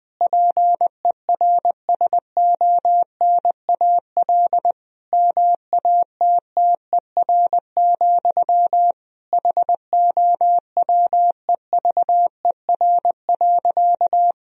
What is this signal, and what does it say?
Morse code, heard as a keyed tone, about 20 wpm, 700 Hz. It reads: PERSONAL MATTER, HOWEVER.
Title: Sentences from Top 1000 Words - Review Part 4 - 20wpm